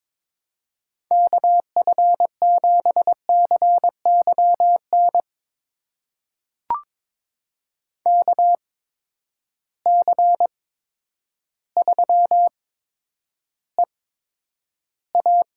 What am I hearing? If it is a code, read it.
Morse code, heard as a keyed tone, about 22 wpm, spaced as for 10 wpm, 700 Hz. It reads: KF7CYN KC3EA
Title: USA Call Signs - 6 Characters - 8x Character Spacing - 22wpm